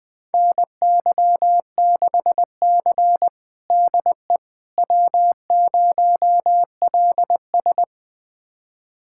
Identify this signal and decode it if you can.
Morse code, heard as a keyed tone, about 20 wpm, 700 Hz. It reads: NY6C DE W0LS
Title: USA Callsign DE Callsign - 4 Characters - 20wpm